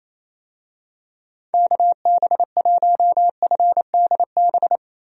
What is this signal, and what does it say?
Morse code, heard as a keyed tone, about 28 wpm, 700 Hz. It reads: KB1FDB
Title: USA Call Signs - 6 Characters - 28wpm